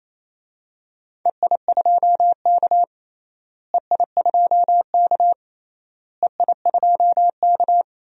Morse code, heard as a keyed tone, about 28 wpm, 700 Hz. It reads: EI2K EI2K EI2K
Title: DXCC-100 Call Signs - 4 Characters - Repeated 3x - 28wpm